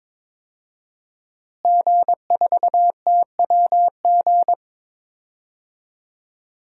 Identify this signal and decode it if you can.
Morse code, heard as a keyed tone, about 22 wpm, 700 Hz. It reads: G4TWG
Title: DXCC-100 Call Signs - 5 Characters - Repeated 2x - 22wpm